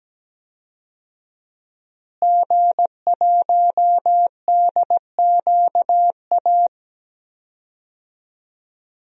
Morse code, heard as a keyed tone, about 17 wpm, 700 Hz. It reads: G1DQA
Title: DXCC-100 Call Signs - 5 Characters - Repeated 2x - 17wpm